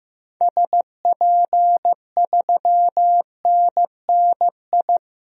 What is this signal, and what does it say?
Morse code, heard as a keyed tone, about 15 wpm, 700 Hz. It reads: SP3NNI
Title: DXCC-100 Call Signs - 6 Characters - 15wpm